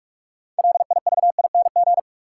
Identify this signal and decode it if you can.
Morse code, heard as a keyed tone, about 46 wpm, 700 Hz. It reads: LIVING
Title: Sets of 4 Words - New Word Drill - 45wpm